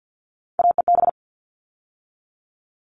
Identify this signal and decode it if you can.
Morse code, heard as a keyed tone, about 50 wpm, 700 Hz. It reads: AEB
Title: All Cut Numbers - 3-Digit 50wpm